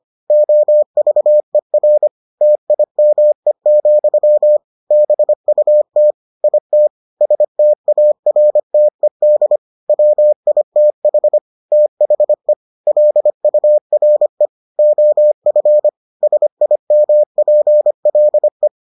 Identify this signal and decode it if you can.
Morse code, heard as a keyed tone, about 25 wpm, 600 Hz. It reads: OVER TIME, BUT IT STARTED WITH THE LURE OF SIMPLE